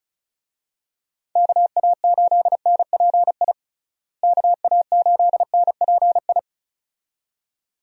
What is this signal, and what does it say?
Morse code, heard as a keyed tone, about 36 wpm, 700 Hz. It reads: KA8NPI KA8NPI
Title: USA Call Signs - 6 Characters - Repeated 2x - 35wpm